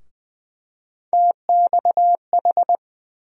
Morse code, heard as a keyed tone, about 20 wpm, 700 Hz. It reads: TXH
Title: USA Call Sign Postfixes - 3 Letters 20wpm